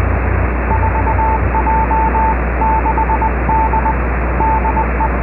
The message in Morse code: 4J6D DE